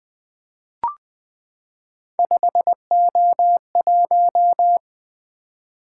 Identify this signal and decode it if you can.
Morse code, heard as a keyed tone, about 20 wpm, 700 Hz. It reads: 5O1
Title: Three Letter-Number Combinations - 20wpm